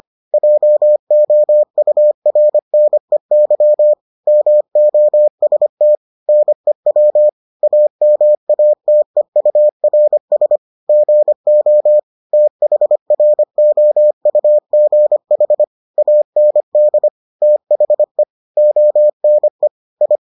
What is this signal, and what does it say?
Morse code, heard as a keyed tone, about 25 wpm, 600 Hz. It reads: JOURNEY MOST NEW AMATEURS GO THROUGH AND THE ONE I